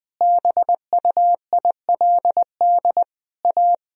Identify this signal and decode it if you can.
Morse code, heard as a keyed tone, about 20 wpm, 700 Hz. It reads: BUILD A
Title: Sentences from Top 1000 Words - Review Part 4 - 20wpm